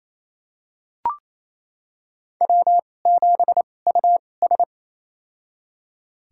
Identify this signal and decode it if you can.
Morse code, heard as a keyed tone, about 28 wpm, spaced as for 20 wpm, 700 Hz. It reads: W7US
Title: USA Call Signs - 4 Characters - 2x Character Spacing - 28wpm